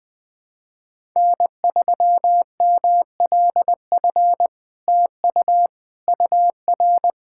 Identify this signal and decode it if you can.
Morse code, heard as a keyed tone, about 20 wpm, 700 Hz. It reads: N3MLF TU UR
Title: SOTA QSO Practice 20wpm